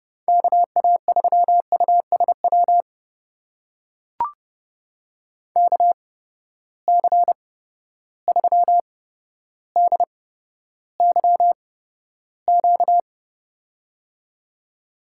Morse code, heard as a keyed tone, about 31 wpm, spaced as for 13 wpm, 700 Hz. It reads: KA3USW KC3DYQ
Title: USA Call Signs - 6 Characters - 8x Character Spacing - 30wpm